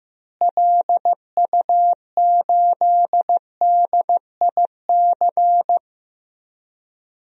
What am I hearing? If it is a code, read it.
Morse code, heard as a keyed tone, about 15 wpm, 700 Hz. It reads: LU8DIC
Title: DXCC-100 Call Signs - 6 Characters - Repeated 2x - 15wpm